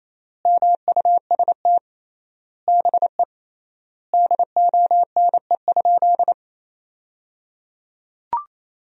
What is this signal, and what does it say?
Morse code, heard as a keyed tone, about 28 wpm, 700 Hz. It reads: MUST BE DONE?
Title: 3x Word-Spacing - Sentences from Top 500 Words - 28wpm